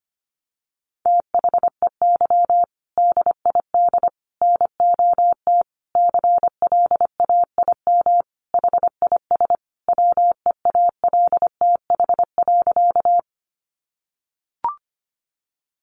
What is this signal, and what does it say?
Morse code, heard as a keyed tone, about 25 wpm, 700 Hz. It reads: THEY DID NOT CLAIM HIS WEALTH.
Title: Sentences from Top 1500 Words 25wpm